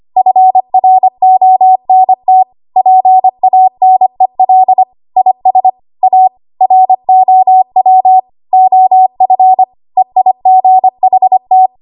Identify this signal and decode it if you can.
Morse code, heard as a keyed tone, about 25 wpm, 750 Hz. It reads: FRONT PANEL IS A ROW OF EIGHT
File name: Archive / 25 WPM / 25WPM